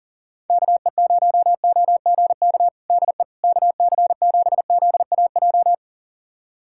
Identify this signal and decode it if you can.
Morse code, heard as a keyed tone, about 40 wpm, 700 Hz. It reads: KE0OGK DE KC7ZAJ
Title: USA Callsign DE Callsign - 6 Characters - 40wpm